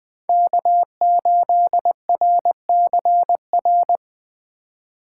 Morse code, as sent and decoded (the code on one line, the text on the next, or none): -.- ---.. .-. -.-. .-.
K8RCR